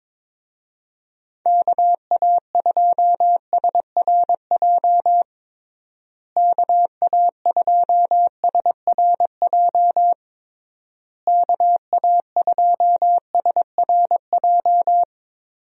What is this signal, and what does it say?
Morse code, heard as a keyed tone, about 22 wpm, 700 Hz. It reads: KA2SRJ KA2SRJ KA2SRJ
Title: USA Call Signs - 6 Characters - Repeated 3x - 22wpm